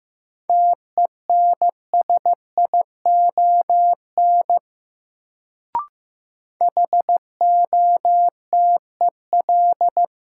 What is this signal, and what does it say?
Morse code, heard as a keyed tone, about 15 wpm, 700 Hz. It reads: TENSION HOTEL
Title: Sentences from Top 1600 Words - New Word Drill Alt - 15wpm